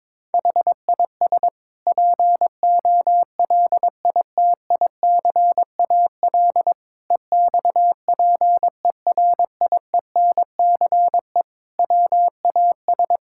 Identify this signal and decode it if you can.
Morse code, heard as a keyed tone, about 22 wpm, 700 Hz. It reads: HIS POLITICAL EXPERIENCE WAS
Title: Sentences from Top 1000 Words - Review Part 6 - 22wpm